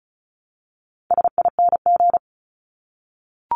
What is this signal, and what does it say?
Morse code, heard as a keyed tone, about 34 wpm, 700 Hz. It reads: SING
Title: Top 500 Words Repeated 35wpm